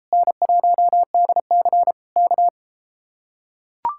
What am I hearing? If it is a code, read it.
Morse code, heard as a keyed tone, about 33 wpm, 700 Hz. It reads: N1DC K